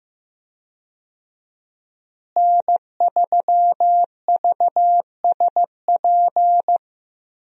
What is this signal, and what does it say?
Morse code, heard as a keyed tone, about 16 wpm, 700 Hz. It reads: N3VSP
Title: USA Call Signs - 5 Characters - Repeated 2x - 15wpm